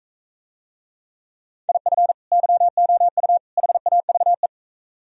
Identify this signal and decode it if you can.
Morse code, heard as a keyed tone, about 43 wpm, 700 Hz. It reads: IF YOU HAVE